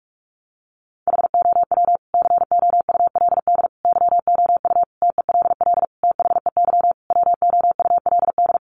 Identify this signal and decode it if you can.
Morse code, heard as a keyed tone, about 44 wpm, 700 Hz. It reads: HOW COULD YOU TELL THEY WOULD
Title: Sentences from Top 1000 Words 45wpm